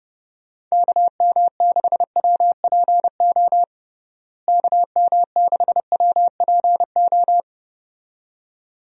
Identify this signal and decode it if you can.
Morse code, heard as a keyed tone, about 29 wpm, 700 Hz. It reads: KM6WPO KM6WPO
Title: USA Call Signs - 6 Characters - Repeated 2x - 30wpm